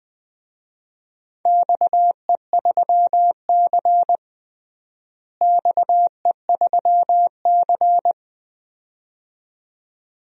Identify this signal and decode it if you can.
Morse code, heard as a keyed tone, about 21 wpm, 700 Hz. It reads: XE3C XE3C
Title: DXCC-100 Call Signs - 4 Characters - Repeated 2x - 20wpm